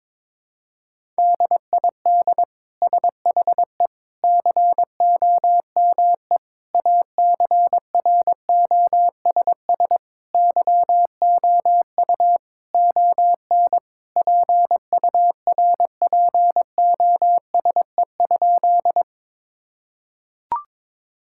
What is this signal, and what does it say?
Morse code, heard as a keyed tone, about 22 wpm, 700 Hz. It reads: DID SHE COME ACROSS YOU ON PURPOSE?
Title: Sentences from Top 1000 Words - Review Part 5 - 22wpm